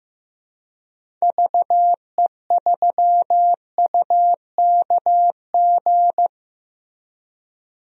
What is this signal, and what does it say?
Morse code, heard as a keyed tone, about 15 wpm, 700 Hz. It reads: VE3UKG
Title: DXCC-100 Call Signs - 6 Characters - Repeated 3x - 15wpm